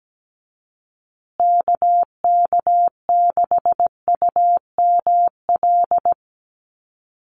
Morse code, heard as a keyed tone, about 17 wpm, 700 Hz. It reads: KK6UML
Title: USA Call Signs Repeated 3x 17wpm